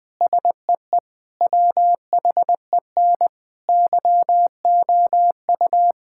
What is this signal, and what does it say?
Morse code, heard as a keyed tone, about 20 wpm, 700 Hz. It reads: SEE WHEN YOU
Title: Sentences with 4-Word Phrases - 20wpm